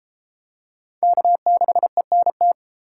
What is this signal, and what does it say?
Morse code, heard as a keyed tone, about 31 wpm, 700 Hz. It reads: K6ENT